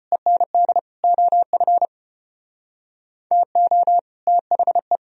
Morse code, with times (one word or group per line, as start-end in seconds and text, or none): 0.1-0.8 END
1.0-1.9 OF
3.3-4.0 TO
4.3-5.0 THE